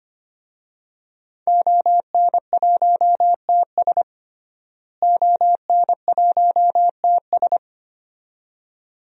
Morse code, heard as a keyed tone, about 25 wpm, 700 Hz. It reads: ON1TS ON1TS
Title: DXCC-100 Call Signs - 5 Characters - Repeated 2x - 25wpm